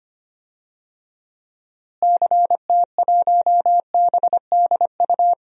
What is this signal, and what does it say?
Morse code, heard as a keyed tone, about 25 wpm, 700 Hz. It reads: CT1BDU
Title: DXCC-100 Call Signs - 6 Characters - Repeated 2x - 25wpm